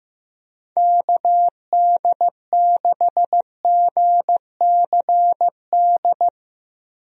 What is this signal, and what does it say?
Morse code, heard as a keyed tone, about 15 wpm, 700 Hz. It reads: KD6GCD
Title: USA Call Signs - 6 Characters - Repeated 2x - 15wpm